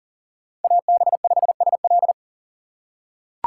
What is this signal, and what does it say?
Morse code, heard as a keyed tone, about 40 wpm, 700 Hz. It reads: AB5SL
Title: USA Call Signs - 5 Characters - 6x Character Spacing - 40wpm